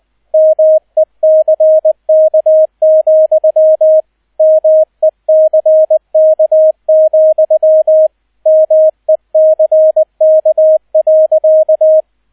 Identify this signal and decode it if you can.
Morse code, heard as a keyed tone, about 20 wpm, 630 Hz, with radio noise. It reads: MECK, MECK, MECK.